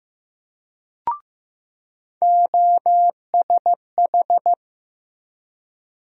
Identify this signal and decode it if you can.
Morse code, heard as a keyed tone, about 15 wpm, 700 Hz. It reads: OSH
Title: USA Call Sign Postfixes - 3 Letters 15wpm